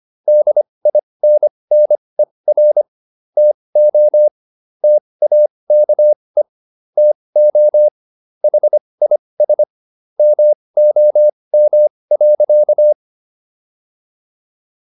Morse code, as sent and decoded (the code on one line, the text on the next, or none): -.. .. -. -. .-. / - --- / - .- -.- . / - --- / .... .. ... / -- --- -- .-.-.-
DINNER TO TAKE TO HIS MOM.